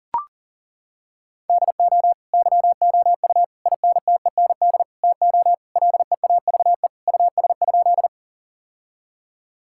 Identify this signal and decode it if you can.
Morse code, heard as a keyed tone, about 40 wpm, 700 Hz. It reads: DO YOU INTEND TO LEAVE US?
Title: Sentences from Top 1600 Words 40wpm